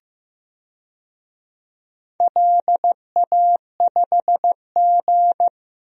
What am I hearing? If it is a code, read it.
Morse code, heard as a keyed tone, about 15 wpm, 700 Hz. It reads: LA5G
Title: DXCC-100 Call Signs - 4 Characters - Repeated 3x - 15wpm